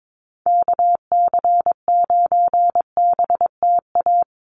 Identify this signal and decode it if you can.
Morse code, heard as a keyed tone, about 22 wpm, 700 Hz. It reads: KC9BTA
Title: USA Call Signs Repeated 2x 22wpm